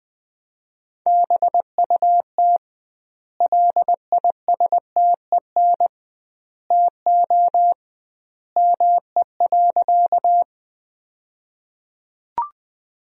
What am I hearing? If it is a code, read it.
Morse code, heard as a keyed tone, about 20 wpm, 700 Hz. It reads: BUT LISTEN TO ME.
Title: 2x Word-Spacing - Sentences from Top 400 Words - 20wpm